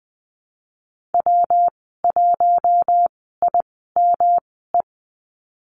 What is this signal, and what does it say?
Morse code, heard as a keyed tone, about 20 wpm, spaced as for 15 wpm, 700 Hz. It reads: W1IME
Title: 122 Lesson 20g - Number 3 - Call Signs 20wpm